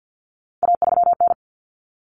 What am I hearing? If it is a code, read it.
Morse code, heard as a keyed tone, about 50 wpm, 700 Hz. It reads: A3N
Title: Common Cut Numbers - 3-Digit 50wpm